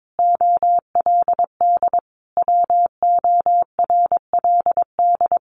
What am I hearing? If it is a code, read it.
Morse code, heard as a keyed tone, about 22 wpm, 700 Hz. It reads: OLD WORLD
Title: Sentences from Top 1300 Words 22wpm